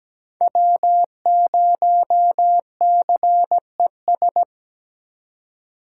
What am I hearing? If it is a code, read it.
Morse code, heard as a keyed tone, about 20 wpm, 700 Hz. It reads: W0CES
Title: Speed Racing - USA 1x3 Call Signs 17wpm